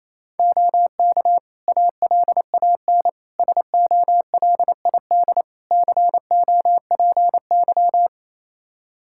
Morse code, heard as a keyed tone, about 28 wpm, 700 Hz. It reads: OK ALAN SOLID COPY
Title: Practice QSO - Exchange 3 - More Info - 28wpm